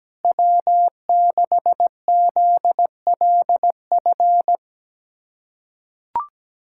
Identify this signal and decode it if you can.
Morse code, heard as a keyed tone, about 17 wpm, 700 Hz. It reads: W6ZLF
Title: USA Callsign DE Callsign - 5 Characters - 17wpm